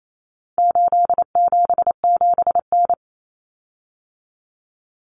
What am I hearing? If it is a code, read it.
Morse code, heard as a keyed tone, about 28 wpm, 700 Hz. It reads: 877N
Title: Common Cut Numbers - 4-Digit 28wpm